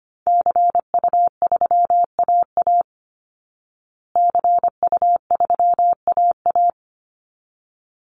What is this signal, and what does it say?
Morse code, heard as a keyed tone, about 25 wpm, 700 Hz. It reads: CU3AA CU3AA
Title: Call Signs Repeated 3x - 25wpm